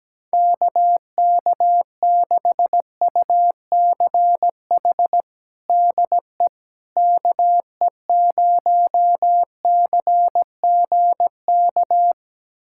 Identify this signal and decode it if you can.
Morse code, heard as a keyed tone, about 17 wpm, 700 Hz. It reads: KK6UCH DE KE0CGK
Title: USA Callsign DE Callsign 17wpm